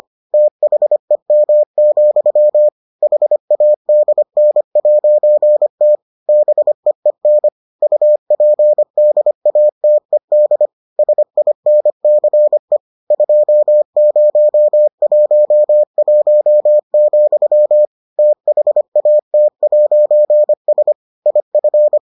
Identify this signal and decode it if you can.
Morse code, heard as a keyed tone, about 25 wpm, 600 Hz. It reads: THEM, HADN'T BEEN UPDATED SINCE 2011, THAT'S IF